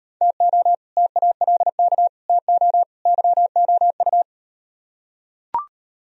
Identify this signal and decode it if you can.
Morse code, heard as a keyed tone, about 37 wpm, 700 Hz. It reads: TO TALK TO YOU